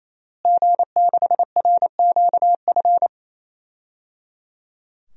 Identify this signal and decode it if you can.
Morse code, heard as a keyed tone, about 27 wpm, 700 Hz. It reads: G6RQF
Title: DXCC-100 Call Signs - 5 Characters - Repeated 3x - 28wpm